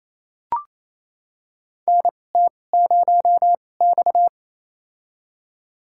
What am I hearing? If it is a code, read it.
Morse code, heard as a keyed tone, about 28 wpm, spaced as for 20 wpm, 700 Hz. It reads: NT0X
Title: USA Call Signs - 4 Characters - 2x Character Spacing - 28wpm